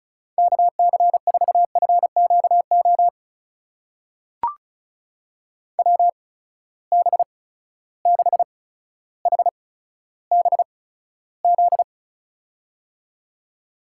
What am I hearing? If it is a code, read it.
Morse code, heard as a keyed tone, about 36 wpm, spaced as for 15 wpm, 700 Hz. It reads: KC4FQO WB6HBZ
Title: USA Call Signs - 6 Characters - 8x Character Spacing - 35wpm